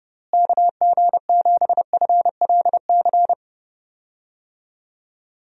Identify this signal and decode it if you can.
Morse code, heard as a keyed tone, about 30 wpm, 700 Hz. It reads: KG7FLC
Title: USA Call Signs - 6 Characters - Repeated 2x - 30wpm